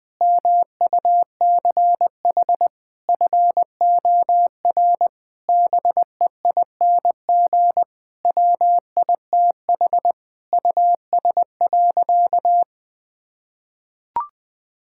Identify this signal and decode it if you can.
Morse code, heard as a keyed tone, about 20 wpm, 700 Hz. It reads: MUCH FOR BEING WITH US.
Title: Sentences with 5-Word Phrases - Spoken Beginning - 20wpm